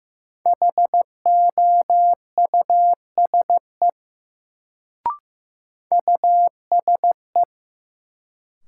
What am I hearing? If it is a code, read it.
Morse code, heard as a keyed tone, about 15 wpm, 700 Hz. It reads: HOUSE USE
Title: Sets of 2 Words - New Word Drill - 15wpm